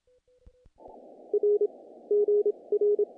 Morse code, heard as a keyed tone, about 26 wpm, spaced as for 22 wpm, 410 Hz. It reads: R GR
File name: contests / naqp-cw-k5tr-op-k5pi- -1800